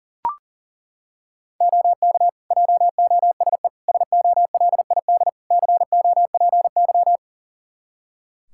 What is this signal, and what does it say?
Morse code, heard as a keyed tone, about 40 wpm, 700 Hz. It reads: OK JOSE SOLID COPY
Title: Practice QSO - Exchange 3 - More Info - 40wpm